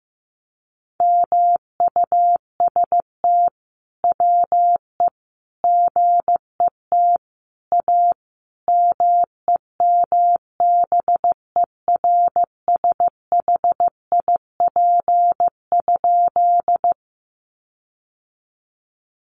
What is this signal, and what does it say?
Morse code, heard as a keyed tone, about 15 wpm, 700 Hz. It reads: MUST WE GET A MEMBERSHIP?
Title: Sentences from Top 1400 Words 15wpm